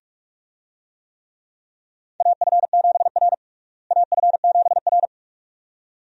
Morse code, heard as a keyed tone, about 43 wpm, 700 Hz. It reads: AF7R AF7R
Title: USA Call Signs - 4 Characters - Repeated 2x - 45wpm